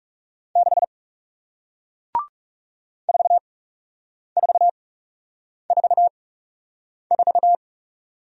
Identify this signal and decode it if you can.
Morse code, heard as a keyed tone, about 40 wpm, spaced as for 8 wpm, 700 Hz. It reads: 6 4444